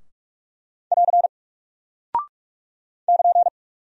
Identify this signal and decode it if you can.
Morse code, heard as a keyed tone, about 44 wpm, 700 Hz. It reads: <AR> <KN>
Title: Single Character (Letter:Number:Punctuation:Prosign) 45wpm